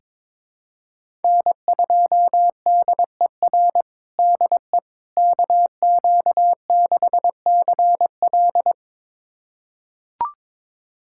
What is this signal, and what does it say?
Morse code, heard as a keyed tone, about 22 wpm, 700 Hz. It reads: N2DER DE KQ6CL
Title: USA Callsign DE Callsign - 5 Characters - 22wpm